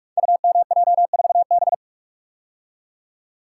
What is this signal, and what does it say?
Morse code, heard as a keyed tone, about 44 wpm, 700 Hz. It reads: UMJ4B
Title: Code Groups - 5 Letter:Numbers - 45wpm